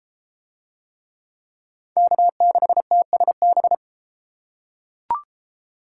Morse code, heard as a keyed tone, about 30 wpm, 700 Hz. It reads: K6TSB